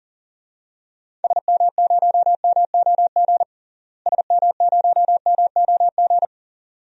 Morse code, heard as a keyed tone, about 40 wpm, 700 Hz. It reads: SM0MOG SM0MOG
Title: DXCC-100 Call Signs - 6 Characters - Repeated 2x - 40wpm